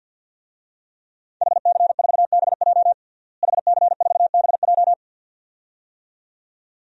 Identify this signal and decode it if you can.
Morse code, heard as a keyed tone, about 49 wpm, 700 Hz. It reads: HC4BJ HC4BJ
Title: DXCC-100 Call Signs - 5 Characters - Repeated 2x - 50wpm